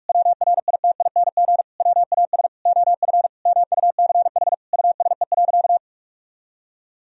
Morse code, heard as a keyed tone, about 41 wpm, 700 Hz. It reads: WRITING WAS OF MUCH USE.
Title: Sentences from Top 1000 Words - Review Part 3 - 45wpm